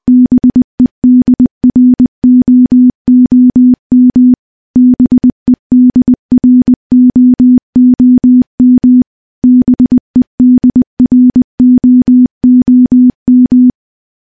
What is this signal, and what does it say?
Morse code, heard as a keyed tone, about 20 wpm, 250 Hz. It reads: BEDROOM BEDROOM BEDROOM
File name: morse-training-20WPM